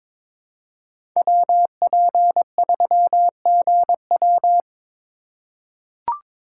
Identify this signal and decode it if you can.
Morse code, heard as a keyed tone, about 21 wpm, 700 Hz. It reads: WP3GW
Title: USA Call Signs - 3x Character Spacing - 22wpm